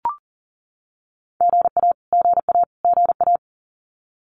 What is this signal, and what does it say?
Morse code, heard as a keyed tone, about 40 wpm, 700 Hz. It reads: GA GA GA
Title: US States Repeated 3x 40wpm